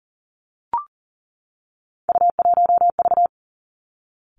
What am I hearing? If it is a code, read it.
Morse code, heard as a keyed tone, about 38 wpm, 700 Hz. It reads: U1V
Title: Three Challenging Letter-Number Combinations 40wpm